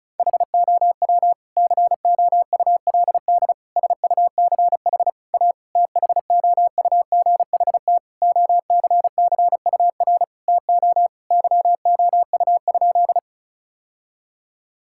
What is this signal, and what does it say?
Morse code, heard as a keyed tone, about 35 wpm, 700 Hz. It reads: HOW COULD SUCH A THOUGHT OCCUR TO YOU?
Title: Sentences from Top 1000 Words - Review Part 6 - 35wpm